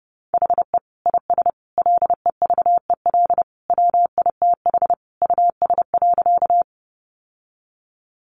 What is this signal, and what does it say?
Morse code, heard as a keyed tone, about 30 wpm, 700 Hz. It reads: HE IS LEVEL WITH US.
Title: Sentences from Top 700 Words 30wpm